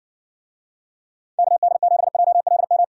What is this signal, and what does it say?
Morse code, heard as a keyed tone, about 60 wpm, 700 Hz. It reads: KD7JLM